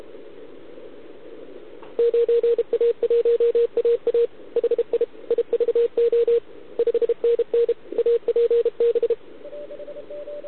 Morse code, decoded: 9A1AA HI IVO 5NN APB